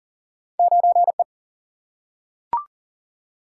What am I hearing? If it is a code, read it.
Morse code, heard as a keyed tone, about 40 wpm, 700 Hz. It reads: 9E
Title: Single-Letter Single-Number 40wpm